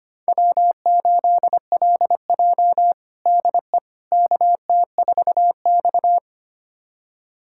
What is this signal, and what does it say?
Morse code, heard as a keyed tone, about 25 wpm, 700 Hz. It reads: W8LJ DE KT4X
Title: USA Callsign DE Callsign - 4 Characters - 25wpm